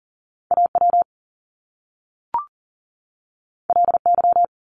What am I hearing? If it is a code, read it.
Morse code, heard as a keyed tone, about 40 wpm, 700 Hz. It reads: AW LY
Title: Top Two Letter Combinations in Words 40wpm